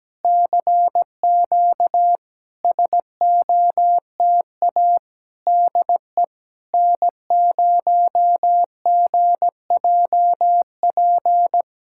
Morse code, heard as a keyed tone, about 17 wpm, 690 Hz. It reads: CQ SOTA DE N0GJP
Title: SOTA QSO Practice 17wpm